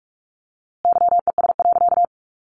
Morse code, heard as a keyed tone, about 44 wpm, 700 Hz. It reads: YES.
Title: A Sound Of Thunder 45wpm